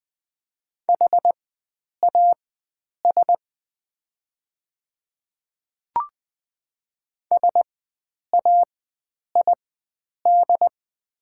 Morse code, heard as a keyed tone, about 20 wpm, spaced as for 9 wpm, 700 Hz. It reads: HAS SAID
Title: 4x Character Spacing - Top 100 Words - 20wpm